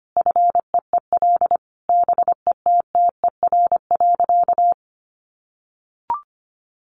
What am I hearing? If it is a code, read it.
Morse code, heard as a keyed tone, about 25 wpm, 700 Hz. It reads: FEEL BETTER.
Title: Easy Phrases 25wpm